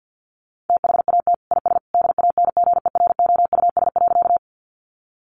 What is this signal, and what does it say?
Morse code, heard as a keyed tone, about 50 wpm, 700 Hz. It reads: THAT IS DANGEROUS.
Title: Sentences from Top 1200 Words 50wpm